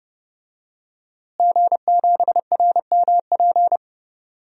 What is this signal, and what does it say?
Morse code, heard as a keyed tone, about 30 wpm, 700 Hz. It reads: G7RMP
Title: DXCC-100 Call Signs - 5 Characters - Repeated 3x - 30wpm